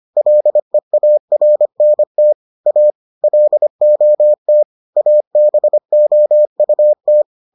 Morse code, heard as a keyed tone, about 25 wpm, 600 Hz. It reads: LEARNT A LOT ABOUT